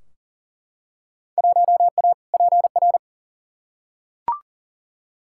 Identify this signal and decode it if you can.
Morse code, heard as a keyed tone, about 40 wpm, 700 Hz. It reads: 1A PR
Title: Field Day Exchanges Encore - 40wpm